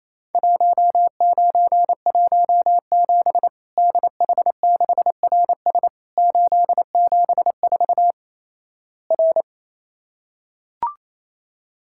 Morse code, heard as a keyed tone, about 28 wpm, 700 Hz. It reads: 1917 DH6RS 874 R
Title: Worked All Europe DX Contest - QTC Exchange - 28wpm